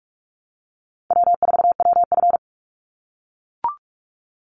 Morse code, heard as a keyed tone, about 44 wpm, 700 Hz. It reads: W4WF
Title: Call Signs - 45wpm